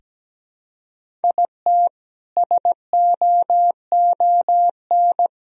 Morse code, heard as a keyed tone, about 17 wpm, 700 Hz. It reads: IT SOON
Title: Sentences from Top 1000 Words - Review Part 1 - 17wpm